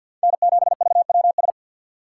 Morse code, heard as a keyed tone, about 53 wpm, 700 Hz. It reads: N7VWS
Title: USA Call Signs - 4x Character Spacing - 50wpm